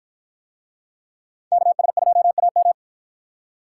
Morse code, heard as a keyed tone, about 53 wpm, 700 Hz. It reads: KS2AM